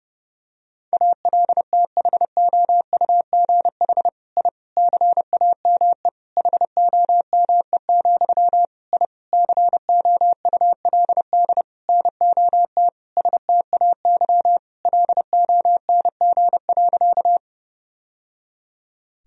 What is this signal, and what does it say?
Morse code, heard as a keyed tone, about 30 wpm, 700 Hz. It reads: ALTHOUGH I CAME HOME, I COULD NOT STAY LONG.
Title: Sentences from Top 1000 Words - Review Part 3 - 30wpm